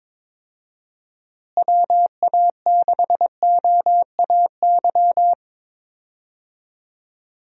Morse code, heard as a keyed tone, about 22 wpm, 700 Hz. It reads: WA6OAY
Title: USA Call Signs - 6 Characters - Repeated 2x - 22wpm